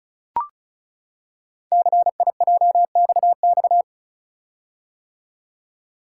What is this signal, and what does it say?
Morse code, heard as a keyed tone, about 36 wpm, 700 Hz. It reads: CIJXX
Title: Code Groups - 5 Letters - 35wpm